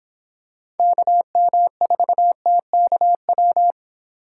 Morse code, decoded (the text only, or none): KM4TKW